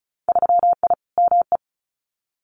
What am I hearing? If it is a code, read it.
Morse code, heard as a keyed tone, about 36 wpm, 700 Hz. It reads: 3I ME
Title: Winter Field Day Exchanges 35wpm